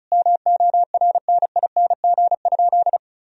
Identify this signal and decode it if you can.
Morse code, heard as a keyed tone, about 34 wpm, 700 Hz. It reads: MORNING?
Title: Sentences from Top 1000 Words - Review Part 2 - 35wpm